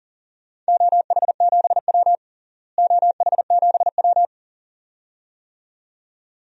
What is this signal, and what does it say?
Morse code, heard as a keyed tone, about 40 wpm, 700 Hz. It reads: OH7W OH7W
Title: DXCC-100 Call Signs - 4 Characters - Repeated 2x - 40wpm